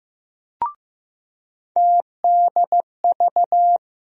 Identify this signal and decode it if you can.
Morse code, heard as a keyed tone, about 15 wpm, 700 Hz. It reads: TDV
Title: USA Call Sign Postfixes - 3 Letters 15wpm